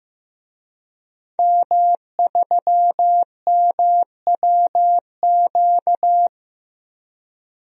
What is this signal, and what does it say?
Morse code, heard as a keyed tone, about 16 wpm, 700 Hz. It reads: M3MWQ
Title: DXCC-100 Call Signs - 5 Characters - 15wpm